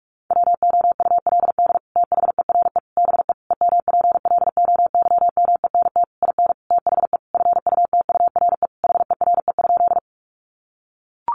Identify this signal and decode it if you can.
Morse code, heard as a keyed tone, about 44 wpm, 700 Hz. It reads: WOULD THERE BE EMPLOYMENT IN THE FUTURE HERE?
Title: Sentences from Top 1500 Words 45wpm